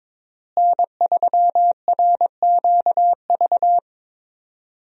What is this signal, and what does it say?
Morse code, heard as a keyed tone, about 21 wpm, 700 Hz. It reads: N3RQV
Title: USA Call Signs - 5 Characters - Repeated 3x - 22wpm